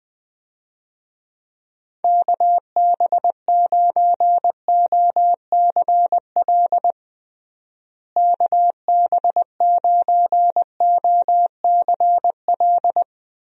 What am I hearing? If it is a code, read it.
Morse code, heard as a keyed tone, about 21 wpm, 700 Hz. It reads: KB9OCL KB9OCL
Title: USA Call Signs - 6 Characters - Repeated 3x - 20wpm